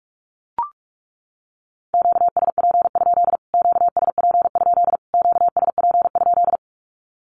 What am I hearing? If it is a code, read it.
Morse code, heard as a keyed tone, about 44 wpm, 700 Hz. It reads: QSP? QSP? QSP?
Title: Q-Codes Repeated 3x 45wpm